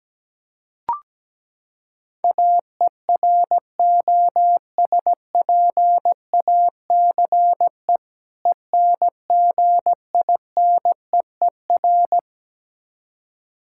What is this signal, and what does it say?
Morse code, heard as a keyed tone, about 17 wpm, 700 Hz. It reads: AEROSPACE ENGINEER
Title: Practice QSO - Exchange 3 - More Info - 17wpm